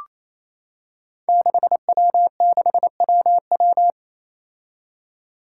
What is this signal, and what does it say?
Morse code, heard as a keyed tone, about 27 wpm, 700 Hz. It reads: 6W6WW
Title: Five Challenging Letter-Number Combinations 28wpm